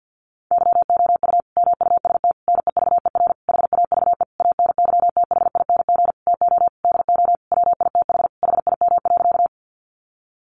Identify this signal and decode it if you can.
Morse code, heard as a keyed tone, about 50 wpm, 700 Hz. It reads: YOU MUST NEVER HAVE ANYTHING TO DO WITH HIM.
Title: Sentences from Top 1000 Words 50wpm